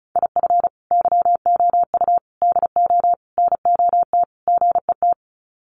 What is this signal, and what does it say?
Morse code, heard as a keyed tone, about 34 wpm, 700 Hz. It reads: IF YOU DO NOT GET
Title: Sets of 5 Words Encore2 35wpm